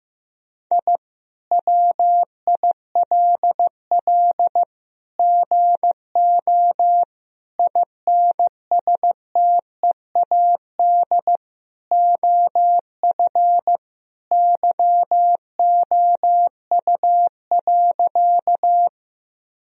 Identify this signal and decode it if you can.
Morse code, heard as a keyed tone, about 16 wpm, 700 Hz. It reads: I WILL GO INSTEAD OF YOU.
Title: Sentences from Top 1000 Words - Review Part 4 - 15wpm